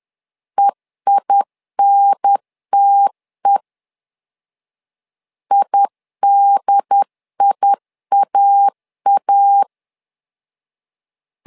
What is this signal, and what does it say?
Morse code, heard as a keyed tone, about 10 wpm, 800 Hz. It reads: EINTE IDIAA
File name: LEZIONI / 2 UHDN / 2 UHDN 10wpm 1904